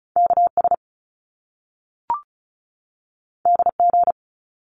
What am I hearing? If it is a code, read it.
Morse code, heard as a keyed tone, about 36 wpm, 700 Hz. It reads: KS DG
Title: Top Two Letter Combinations in Words 35wpm